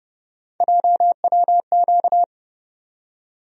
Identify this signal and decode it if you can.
Morse code, heard as a keyed tone, about 32 wpm, 700 Hz. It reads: JWQ
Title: USA Call Sign Postfixes - 3 Letters 30wpm